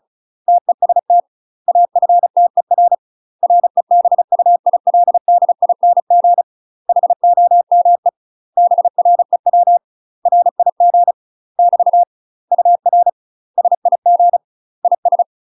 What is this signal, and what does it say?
Morse code, heard as a keyed tone, about 36 wpm, 700 Hz. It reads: TEST AFTER REBUILDING HOME BREW RIG <BT> UR SIG IS
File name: UK MASTER QSOs TRAINING / 35WPM / UK MASTER QSOs 0036